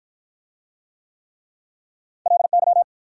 Morse code, heard as a keyed tone, about 54 wpm, 700 Hz. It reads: LY